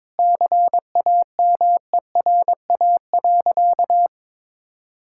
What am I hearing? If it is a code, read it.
Morse code, heard as a keyed tone, about 22 wpm, 700 Hz. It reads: CAMERA.
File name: practice / 22wpm / Sentences with 3-Word Phrases - 22wpm